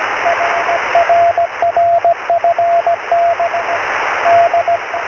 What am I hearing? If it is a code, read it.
Morse code, heard as a keyed tone, about 17 wpm, 680 Hz, with radio noise. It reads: I RRFN D
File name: sm7lcb / 06-oz6ol